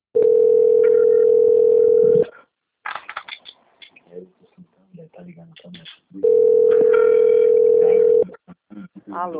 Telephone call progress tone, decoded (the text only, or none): ringback tone